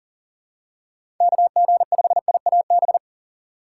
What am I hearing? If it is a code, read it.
Morse code, heard as a keyed tone, about 39 wpm, 700 Hz. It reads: KG5IAB
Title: USA Call Signs - 6 Characters - 4x Character Spacing - 40wpm